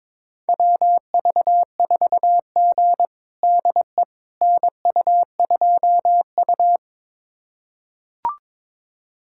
Morse code, heard as a keyed tone, about 22 wpm, 700 Hz. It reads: WV4G DE NU2U
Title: USA Callsign DE Callsign - 4 Characters - 22wpm